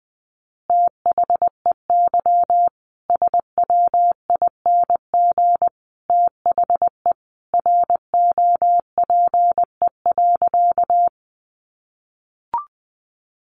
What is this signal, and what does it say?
Morse code, heard as a keyed tone, about 19 wpm, 700 Hz. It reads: THEY SWING THE ROPE.
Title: Easy Phrases 20wpm